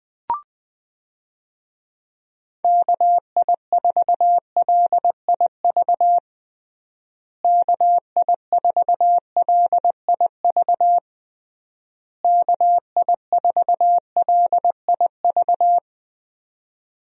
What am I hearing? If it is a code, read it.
Morse code, heard as a keyed tone, about 20 wpm, 700 Hz. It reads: KI4LIV KI4LIV KI4LIV
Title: USA Call Signs - 6 Characters - Repeated 3x - 20wpm